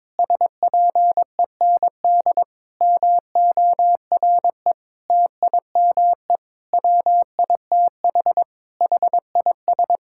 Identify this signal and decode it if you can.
Morse code, heard as a keyed tone, about 22 wpm, 700 Hz. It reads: SPEND MORE TIME WITH HIS
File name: Sentences with 7-Word Phrases - 22wpm